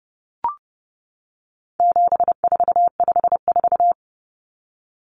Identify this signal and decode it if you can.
Morse code, heard as a keyed tone, about 30 wpm, 700 Hz. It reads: 7454
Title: Numbers - 4-Digit 30wpm